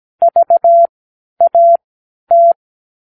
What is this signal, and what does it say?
Morse code, heard as a keyed tone, about 17 wpm, spaced as for 10 wpm, 680 Hz. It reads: VAT
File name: k7qo / Practicewords